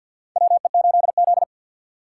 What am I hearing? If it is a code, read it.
Morse code, heard as a keyed tone, about 51 wpm, 700 Hz. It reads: WE8Z
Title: USA Call Signs - 4 Characters - 3x Character Spacing - 50wpm